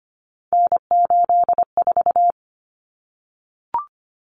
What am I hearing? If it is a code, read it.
Morse code, heard as a keyed tone, about 25 wpm, 700 Hz. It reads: N84
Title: Common Cut Numbers - 3-Digit 25wpm